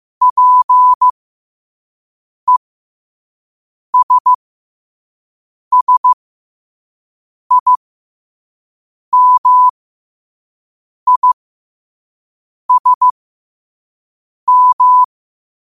Morse code, heard as a keyed tone, about 15 wpm, spaced as for 5 wpm, 1000 Hz. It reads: PESSIMISM
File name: qotdfiles / 5wpm